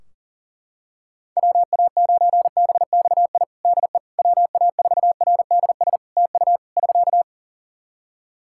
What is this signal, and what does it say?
Morse code, heard as a keyed tone, about 40 wpm, 700 Hz. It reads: WA9BXI DE WA4RDS TU <SK>
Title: Practice QSO - Exchange 4 - Ending - 40wpm